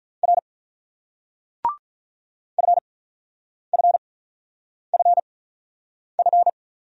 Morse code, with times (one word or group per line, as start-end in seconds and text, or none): 0.2-0.4 R
2.6-6.5 FFFF